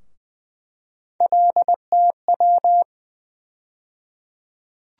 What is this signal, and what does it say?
Morse code, heard as a keyed tone, about 20 wpm, 700 Hz. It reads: LTW
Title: USA Call Sign Postfixes - 3 Letters 20wpm